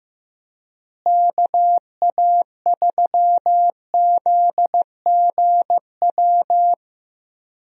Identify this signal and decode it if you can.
Morse code, heard as a keyed tone, about 15 wpm, 700 Hz. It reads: KA3ZGW
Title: USA Call Signs - 6 Characters - Repeated 3x - 15wpm